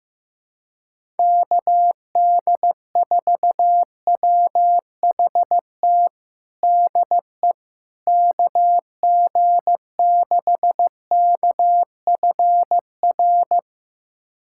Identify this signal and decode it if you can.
Morse code, heard as a keyed tone, about 15 wpm, 700 Hz. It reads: KD4WHT DE KG6KFR
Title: USA Callsign DE Callsign - 6 Characters - 15wpm